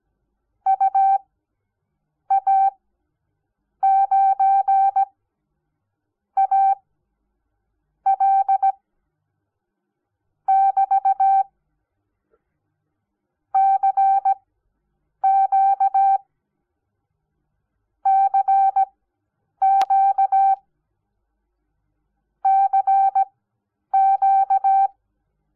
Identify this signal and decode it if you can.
Morse code, heard as a keyed tone, about 17 wpm, spaced as for 7 wpm, 780 Hz. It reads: UA9AL <BT> CQ CQ CQ
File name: Onlinelehrgang / morsen / DJ4UF Morsen06